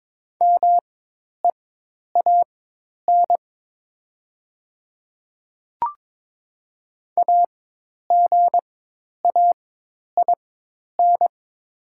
Morse code, heard as a keyed tone, about 22 wpm, spaced as for 10 wpm, 700 Hz. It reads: MEAN AGAIN
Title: 4x Character Spacing - Top 200 Words - 22wpm